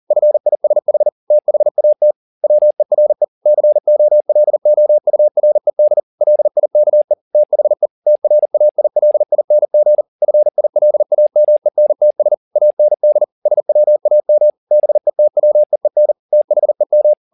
Morse code, heard as a keyed tone, about 40 wpm, 600 Hz. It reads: FISH THAT WERE COLOURED LIKE THE TRAILING FILAMENTS AND SWAM BETWEEN THEM